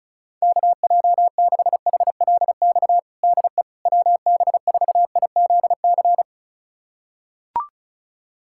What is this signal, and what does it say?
Morse code, heard as a keyed tone, about 36 wpm, 700 Hz. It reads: KJ6HLX DE WB4IZC
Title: USA Callsign DE Callsign - 6 Characters - 35wpm